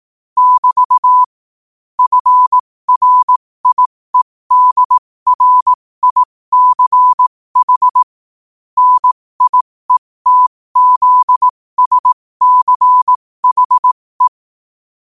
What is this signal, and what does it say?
Morse code, heard as a keyed tone, about 18 wpm, spaced as for 15 wpm, 1000 Hz. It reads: <BT> FRIEDRICH NIETZSCHE
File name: qotdfiles / 15wpm